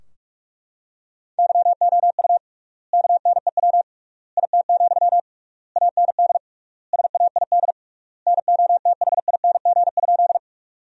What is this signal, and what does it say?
Morse code, heard as a keyed tone, about 41 wpm, 700 Hz. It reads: YOU KNEW IT, AND SAID NOTHING?
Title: 3x Word-Spacing - Sentences from Top 500 Words - 45wpm